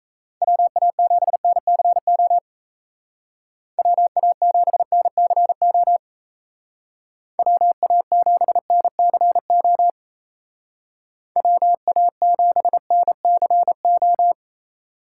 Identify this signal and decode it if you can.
Morse code, heard as a keyed tone, about 37 wpm, 700 Hz. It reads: WA7NCO WA7NCO WA7NCO WA7NCO